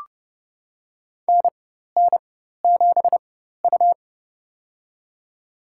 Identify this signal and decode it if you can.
Morse code, heard as a keyed tone, about 30 wpm, spaced as for 14 wpm, 700 Hz. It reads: NN7U
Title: USA Call Signs - 4 Characters - 4x Character Spacing - 30wpm